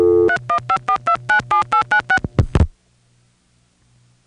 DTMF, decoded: 3154297896